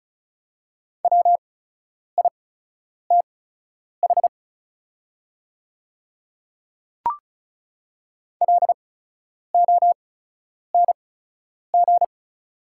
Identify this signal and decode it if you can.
Morse code, heard as a keyed tone, about 36 wpm, spaced as for 10 wpm, 700 Hz. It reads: WITH LONG
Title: 8x Character Spacing - Top 300 Words - 35wpm